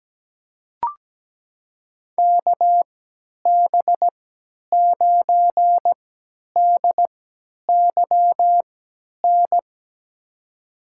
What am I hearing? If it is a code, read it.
Morse code, heard as a keyed tone, about 17 wpm, spaced as for 10 wpm, 700 Hz. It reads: KB9DYN
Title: USA Call Signs - 6 Characters - 3x Character Spacing - 17wpm